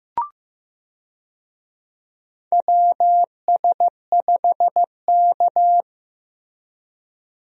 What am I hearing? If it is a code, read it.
Morse code, heard as a keyed tone, about 15 wpm, 700 Hz. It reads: WS5K
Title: USA Call Signs - 4 Characters - Repeated 2x - 15wpm